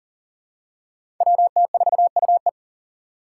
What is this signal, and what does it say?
Morse code, heard as a keyed tone, about 40 wpm, 700 Hz. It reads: WT4UE
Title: Five Challenging Letter-Number Combinations 40wpm